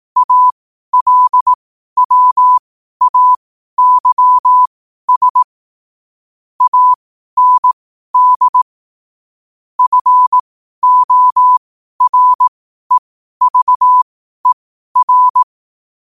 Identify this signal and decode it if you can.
Morse code, heard as a keyed tone, about 18 wpm, spaced as for 12 wpm, 1000 Hz. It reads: ALWAYS AND FOREVER
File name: qotdfiles / 12wpm